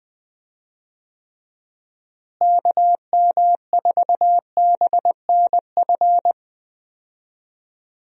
Morse code, heard as a keyed tone, about 21 wpm, 700 Hz. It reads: KM4BNF
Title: USA Call Signs - 6 Characters - Repeated 3x - 20wpm